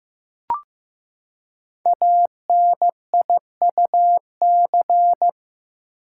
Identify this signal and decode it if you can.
Morse code, heard as a keyed tone, about 15 wpm, 700 Hz. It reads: ANIUC
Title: Code Groups - 5 Letter:Numbers - 15wpm